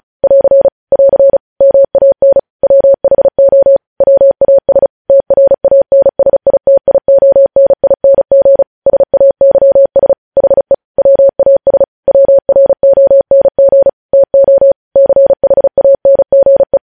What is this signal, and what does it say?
Morse code, heard as a keyed tone, about 36 wpm, 550 Hz. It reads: <AR> <AR> MAN WHO WAS TRANSITIONING SAYS HE WAS WRONG TO CHANGE
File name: podcast / News headlines-1x-35x35